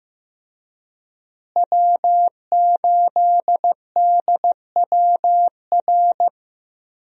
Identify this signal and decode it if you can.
Morse code, heard as a keyed tone, about 15 wpm, 700 Hz. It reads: W8DWR
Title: USA Call Signs - 5 Characters - Repeated 3x - 15wpm